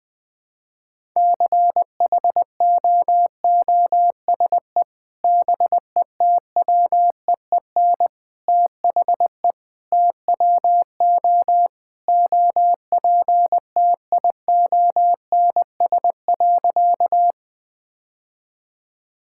Morse code, heard as a keyed tone, about 20 wpm, 700 Hz. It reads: CHOOSE BETWEEN THE TWO OPTIONS.
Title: Sentences with 3-Word Phrases - Spoken Beginning - 20wpm